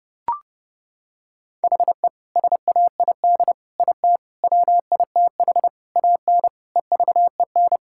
Morse code, heard as a keyed tone, about 30 wpm, 700 Hz. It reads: HE SAID IT WITH AN EVEN
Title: Sentences from Top 1600 Words 30wpm